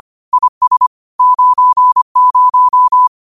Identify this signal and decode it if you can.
Morse code, heard as a keyed tone, about 25 wpm, 1000 Hz. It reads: IS 90
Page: Quotes for Thu, 14 Aug 2025 in Morse Code at 25 words per minute.